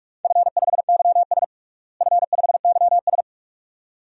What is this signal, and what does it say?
Morse code, heard as a keyed tone, about 46 wpm, 700 Hz. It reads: F5YS F5YS
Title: DXCC-100 Call Signs - 4 Characters - Repeated 2x - 45wpm